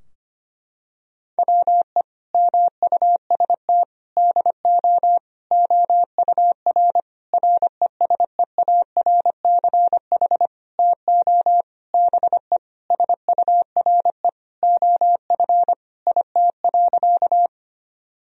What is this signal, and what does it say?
Morse code, heard as a keyed tone, about 25 wpm, 700 Hz. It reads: WE MUST DO OUR RESEARCH TO BE SURE OF IT.
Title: Sentences from Top 1000 Words - Review Part 4 - 25wpm